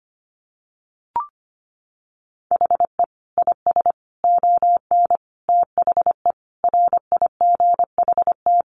Morse code, heard as a keyed tone, about 25 wpm, 690 Hz. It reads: HE IS ON THE RIGHT
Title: Sentences from Top 1100 Words 25wpm